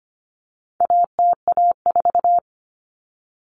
Morse code, heard as a keyed tone, about 25 wpm, 700 Hz. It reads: ATA4
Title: Common Cut Numbers - 4-Digit 25wpm